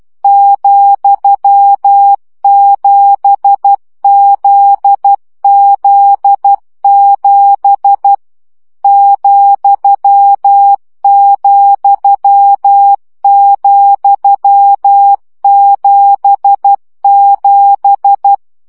Morse code, heard as a keyed tone, about 12 wpm, 800 Hz. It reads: ,7ZZ7 ,,,77
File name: CW gyakorlas z 7 vesszo 10 baud